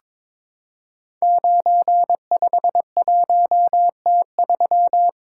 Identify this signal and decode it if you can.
Morse code, heard as a keyed tone, about 23 wpm, 700 Hz. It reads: 951T3
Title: Code Groups - 5 Letter:Numbers - 22wpm